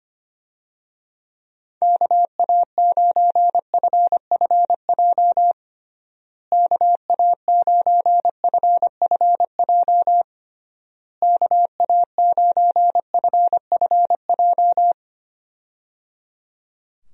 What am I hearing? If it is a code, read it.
Morse code, heard as a keyed tone, about 25 wpm, 700 Hz. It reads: KA9FFJ KA9FFJ KA9FFJ
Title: USA Call Signs - 6 Characters - Repeated 3x - 25wpm